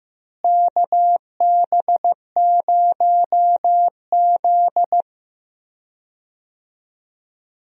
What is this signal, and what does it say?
Morse code, heard as a keyed tone, about 15 wpm, 700 Hz. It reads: KB0Z
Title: USA Call Signs - 4 Characters - Repeated 2x - 15wpm